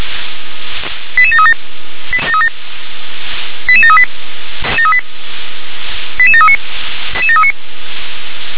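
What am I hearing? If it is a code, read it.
Five-tone selective call (ZVEI), heard as quick runs of five tones, tones of about 70 ms. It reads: 88737 88738 88739
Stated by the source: ZVEI five-tone selective call for code 88666, received by SDR.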